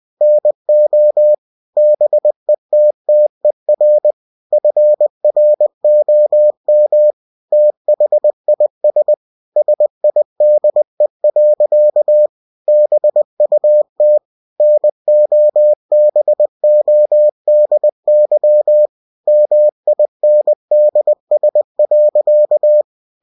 Morse code, heard as a keyed tone, about 20 wpm, 600 Hz. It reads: NO BETTER FROM THIS SIDE. BUT NOBODY MINDS.
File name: cw-books / winnie-the-pooh / winnie-20wpm-page-0044